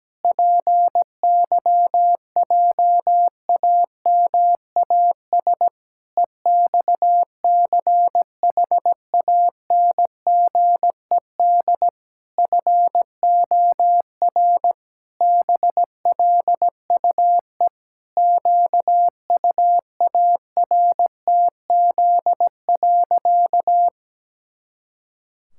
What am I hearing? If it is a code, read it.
Morse code, heard as a keyed tone, about 17 wpm, 700 Hz. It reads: PYJAMAS EXCHANGED FOR BLUE QUARTZ.
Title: Pangram Challenge v2 - 17wpm